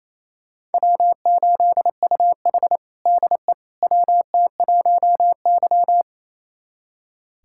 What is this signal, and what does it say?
Morse code, heard as a keyed tone, about 28 wpm, 700 Hz. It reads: W8UH DE WT1Y
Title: USA Callsign DE Callsign - 4 Characters - 28wpm